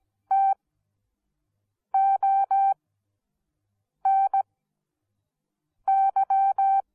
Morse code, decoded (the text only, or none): TONY